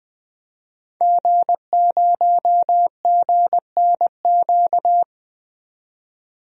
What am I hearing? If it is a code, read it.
Morse code, heard as a keyed tone, about 20 wpm, 700 Hz. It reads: G0GNQ
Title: DXCC-100 Call Signs - 5 Characters - 20wpm